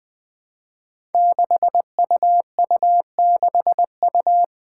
Morse code, heard as a keyed tone, about 20 wpm, 700 Hz. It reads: 6UU6U
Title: Five Challenging Letter-Number Combinations 20wpm